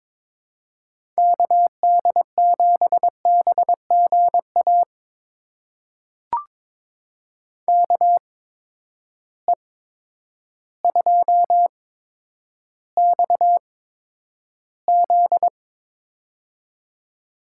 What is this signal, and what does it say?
Morse code, heard as a keyed tone, about 21 wpm, spaced as for 10 wpm, 700 Hz. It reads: KD7BGA KE2XZ
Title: USA Call Signs - 8x Character Spacing - 22wpm